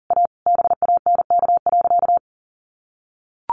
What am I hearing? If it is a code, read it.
Morse code, heard as a keyed tone, about 40 wpm, 700 Hz. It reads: A BANK.
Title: Sentences from Top 1500 Words 40wpm